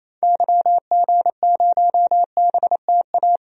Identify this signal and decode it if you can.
Morse code, heard as a keyed tone, about 28 wpm, 700 Hz. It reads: YG0BTA
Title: DXCC-100 Call Signs - 6 Characters - 28wpm